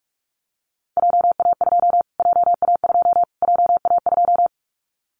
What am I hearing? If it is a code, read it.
Morse code, heard as a keyed tone, about 44 wpm, 700 Hz. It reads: JA2 JA2 JA2
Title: Callsign Prefixes for Top 25 DXCC - Repeated 3x - 45wpm